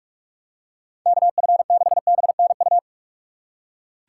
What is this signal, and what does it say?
Morse code, heard as a keyed tone, about 46 wpm, 700 Hz. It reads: KF6BNU
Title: USA Call Signs - 6 Characters - 6x Character Spacing - 45wpm